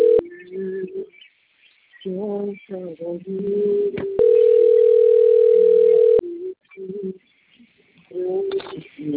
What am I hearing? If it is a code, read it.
Ringback tone (North America), 440 + 480 Hz, with speech.
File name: rg-1600-unknown- - - .66340